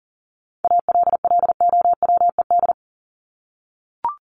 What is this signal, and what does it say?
Morse code, heard as a keyed tone, about 40 wpm, 700 Hz. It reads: ALLOWED
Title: ED Suffix - Top 100 Words with Base Words 40wpm